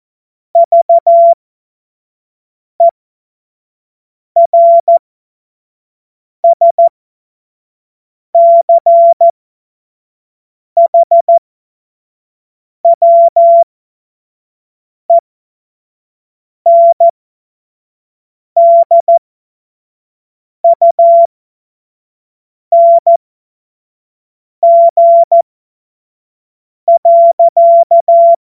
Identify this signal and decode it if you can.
Morse code, heard as a keyed tone, about 14 wpm, spaced as for 5 wpm, 680 Hz. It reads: VERSCHWENDUNG.
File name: Morsekurs / Witze / Witz 31. Mai 2022 25 F70